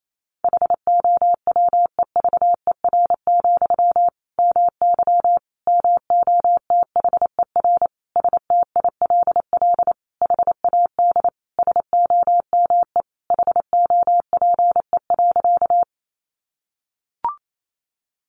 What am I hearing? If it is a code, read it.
Morse code, heard as a keyed tone, about 28 wpm, 700 Hz. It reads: HOWEVER, MY MOTHER STILL HAD SOME HOPE.
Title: Sentences from Top 1000 Words 28wpm